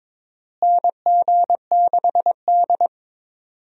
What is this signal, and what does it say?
Morse code, heard as a keyed tone, about 22 wpm, 700 Hz. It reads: NG6D
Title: USA Call Signs - 4 Characters - Repeated 3x - 22wpm